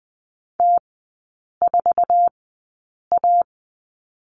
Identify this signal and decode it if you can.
Morse code, heard as a keyed tone, about 20 wpm, spaced as for 8 wpm, 700 Hz. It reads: T4A
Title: 212 Lesson 34e - Number 8 - Three Characters 20wpm